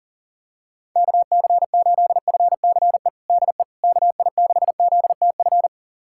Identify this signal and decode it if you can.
Morse code, heard as a keyed tone, about 37 wpm, 700 Hz. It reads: KC8FCE DE KI6ZTF
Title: USA Callsign DE Callsign - 6 Characters - 40wpm